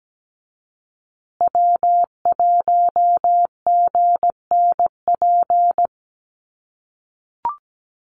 Morse code, heard as a keyed tone, about 17 wpm, 700 Hz. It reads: W1GNP
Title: USA Call Signs Repeated 3x 17wpm